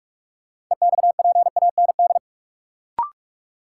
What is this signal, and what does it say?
Morse code, heard as a keyed tone, about 46 wpm, 700 Hz. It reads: EXPAND
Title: Sentences from Top 1400 Words - New Word Drill - 45wpm